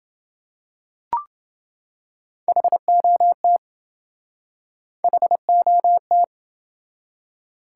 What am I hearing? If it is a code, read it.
Morse code, heard as a keyed tone, about 28 wpm, 700 Hz. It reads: HOT HOT